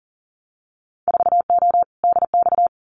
Morse code, heard as a keyed tone, about 40 wpm, 700 Hz. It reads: 4O DX
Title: Winter Field Day Exchanges Encore 40wpm